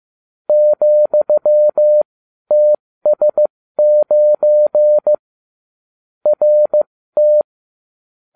Morse code, heard as a keyed tone, about 15 wpm, spaced as for 12 wpm, 600 Hz. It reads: ,TS9 RT